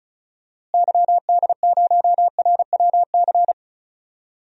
Morse code, heard as a keyed tone, about 36 wpm, 700 Hz. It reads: YD0RWC
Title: DXCC-100 Call Signs - 6 Characters - Repeated 2x - 35wpm